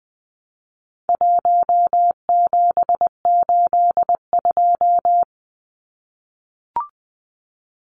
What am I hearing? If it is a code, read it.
Morse code, heard as a keyed tone, about 20 wpm, 700 Hz. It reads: 1782
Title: Numbers - 4-Digit 20wpm